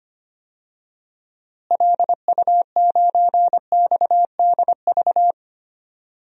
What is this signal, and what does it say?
Morse code, heard as a keyed tone, about 25 wpm, 700 Hz. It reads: LU9XDV
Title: DXCC-100 Call Signs - 6 Characters - Repeated 3x - 25wpm